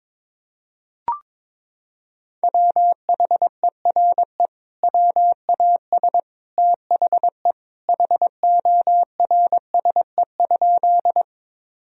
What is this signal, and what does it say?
Morse code, heard as a keyed tone, about 22 wpm, 700 Hz. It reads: WHERE WAS THE HORSE?
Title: Sentences from Top 1000 Words - Review Part 1 - 22wpm